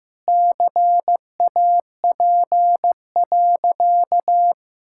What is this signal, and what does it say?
Morse code, heard as a keyed tone, about 16 wpm, 700 Hz. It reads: CAP.
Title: Sentences from Top 1600 Words 15wpm